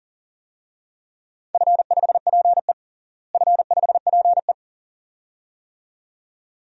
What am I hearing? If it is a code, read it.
Morse code, heard as a keyed tone, about 40 wpm, 700 Hz. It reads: F5PE F5PE
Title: DXCC-100 Call Signs - 4 Characters - Repeated 2x - 40wpm